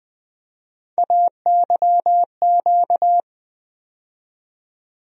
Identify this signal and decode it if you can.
Morse code, heard as a keyed tone, about 20 wpm, 700 Hz. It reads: AYQ
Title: USA Call Sign Postfixes - 3 Letters 20wpm